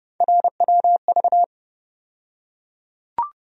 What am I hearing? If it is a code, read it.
Morse code, heard as a keyed tone, about 30 wpm, 700 Hz. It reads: RWV
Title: USA Call Sign Postfixes - 3 Letters 30wpm